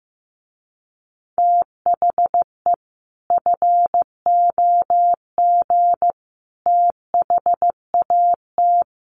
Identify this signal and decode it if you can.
Morse code, heard as a keyed tone, about 15 wpm, 700 Hz. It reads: THE FOG THAT
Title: A Sound Of Thunder 15wpm